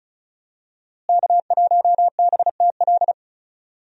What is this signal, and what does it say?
Morse code, heard as a keyed tone, about 34 wpm, 700 Hz. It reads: K1BTL
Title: USA Call Signs - 5 Characters - Repeated 2x - 35wpm